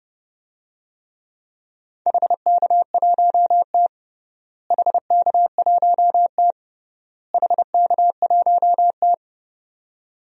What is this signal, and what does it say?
Morse code, heard as a keyed tone, about 30 wpm, 700 Hz. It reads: HK1T HK1T HK1T
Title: DXCC-100 Call Signs - 4 Characters - Repeated 3x - 30wpm